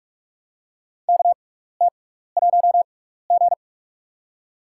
Morse code, heard as a keyed tone, about 41 wpm, spaced as for 15 wpm, 700 Hz. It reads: KT1G
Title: USA Call Signs - 4 Characters - 6x Character Spacing - 45wpm